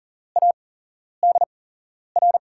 Morse code, heard as a keyed tone, about 40 wpm, spaced as for 11 wpm, 700 Hz. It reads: ADR
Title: USA Call Signs - 5 Characters - 8x Character Spacing - 40wpm